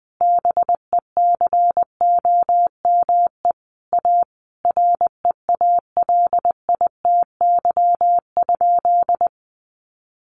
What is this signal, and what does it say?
Morse code, heard as a keyed tone, about 20 wpm, 700 Hz. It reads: BECOME A REALITY?
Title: Sentences from Top 1200 Words 20wpm